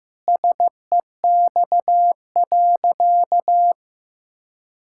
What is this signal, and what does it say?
Morse code, heard as a keyed tone, about 15 wpm, 700 Hz. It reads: SEX.
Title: Sentences from Top 1600 Words 15wpm